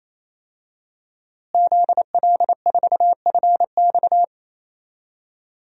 Morse code, heard as a keyed tone, about 28 wpm, 700 Hz. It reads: ZL4FX
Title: DXCC-100 Call Signs - 5 Characters - Repeated 3x - 28wpm